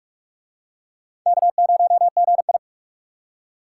Morse code, heard as a keyed tone, about 41 wpm, 700 Hz. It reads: K0GI